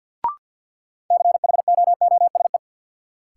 Morse code, heard as a keyed tone, about 48 wpm, 700 Hz. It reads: CHOOSE
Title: Sentences from Top 700 Words - New Word Drill - 50wpm